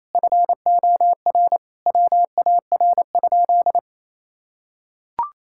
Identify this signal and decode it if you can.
Morse code, heard as a keyed tone, about 29 wpm, 700 Hz. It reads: FOR WAR?
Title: Sentences from Top 1000 Words - Review Part 5 - 28wpm